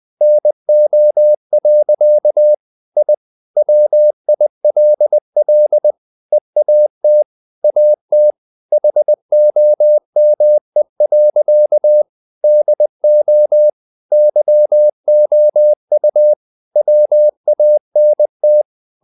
Morse code, heard as a keyed tone, about 20 wpm, 600 Hz. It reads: NO. I WILL EAT AT HOME. DO YOU WANT
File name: the old man and the sea / the old man and the sea part 1 20wpm